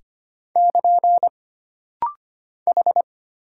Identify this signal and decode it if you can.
Morse code, heard as a keyed tone, about 25 wpm, 700 Hz. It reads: <KN> H
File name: Single Character (Letter:Number:Punctuation:Prosign) - Rapid-Fire - 25wpm